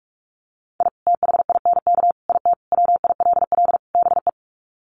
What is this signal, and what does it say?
Morse code, heard as a keyed tone, about 44 wpm, 690 Hz. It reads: I THINK IT WILL BE
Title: Sets of 5 Words Encore2 45wpm